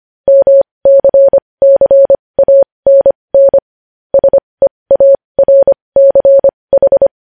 Morse code, heard as a keyed tone, about 25 wpm, spaced as for 20 wpm, 550 Hz. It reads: MCCANN SEARCH
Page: news-headlines-in-morse-code-at-20-wpm